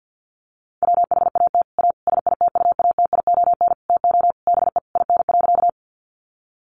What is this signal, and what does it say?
Morse code, heard as a keyed tone, about 50 wpm, 700 Hz. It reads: WHAT A SITUATION TO BE IN.
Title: Sentences from Top 1200 Words 50wpm